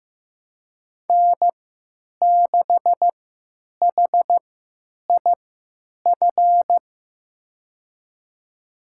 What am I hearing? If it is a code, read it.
Morse code, heard as a keyed tone, about 15 wpm, spaced as for 9 wpm, 700 Hz. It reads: N6HIF
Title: USA Call Signs - 5 Characters - 3x Character Spacing - 15wpm